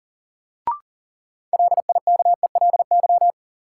Morse code, heard as a keyed tone, about 40 wpm, 700 Hz. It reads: LIKELY
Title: Sets of 2 Words - New Word Drill - 40wpm